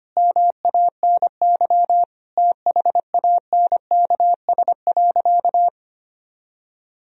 Morse code, heard as a keyed tone, about 25 wpm, 700 Hz. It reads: MANY THANKS.
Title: Sentences from Top 1000 Words - Review Part 3 - 25wpm